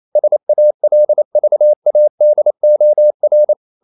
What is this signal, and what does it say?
Morse code, heard as a keyed tone, about 28 wpm, 600 Hz. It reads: SALVADOR
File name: CW / Practice / NationalCapitals28